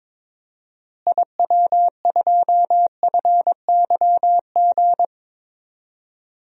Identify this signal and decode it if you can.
Morse code, heard as a keyed tone, about 22 wpm, 700 Hz. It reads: IW2FYG
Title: DXCC-100 Call Signs - 6 Characters - Repeated 3x - 22wpm